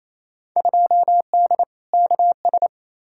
Morse code, heard as a keyed tone, about 28 wpm, 700 Hz. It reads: 2D KS
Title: Field Day Exchanges Encore - 28wpm